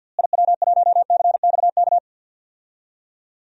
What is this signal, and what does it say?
Morse code, heard as a keyed tone, about 48 wpm, 700 Hz. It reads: IW1CXK
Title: DXCC-100 Call Signs - 6 Characters - Repeated 2x - 50wpm